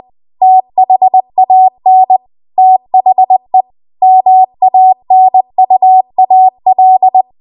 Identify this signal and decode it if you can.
Morse code, heard as a keyed tone, about 20 wpm, 750 Hz. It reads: THAN THE MANUAL
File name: Archive / 20 WPM / 20WPM